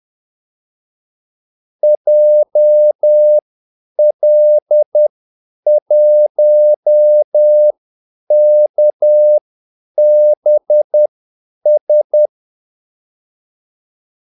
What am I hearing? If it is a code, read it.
Morse code, heard as a keyed tone, about 10 wpm, spaced as for 8 wpm, 600 Hz. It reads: JL1KBS
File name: Callsigns-10-8wpm